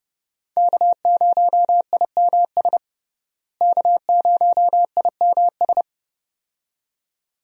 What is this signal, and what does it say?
Morse code, heard as a keyed tone, about 30 wpm, 700 Hz. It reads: K0IMS K0IMS
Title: USA Call Signs - 5 Characters - Repeated 2x - 30wpm